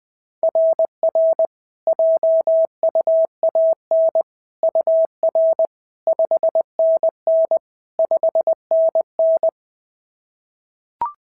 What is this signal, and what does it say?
Morse code, heard as a keyed tone, about 20 wpm, 690 Hz. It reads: RR JUAN UR 5NN 5NN
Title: SOTA QSO Practice 20wpm